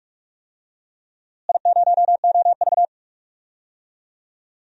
Morse code, heard as a keyed tone, about 42 wpm, 700 Hz. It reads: I0OV
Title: DXCC-100 Call Signs - 4 Characters - Repeated 2x - 45wpm